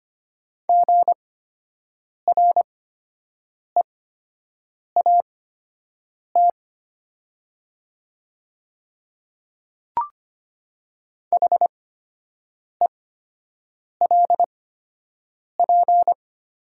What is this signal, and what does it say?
Morse code, heard as a keyed tone, about 25 wpm, spaced as for 7 wpm, 690 Hz. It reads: GREAT HELP
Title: 8x Character Spacing - Top 300 Words - 25wpm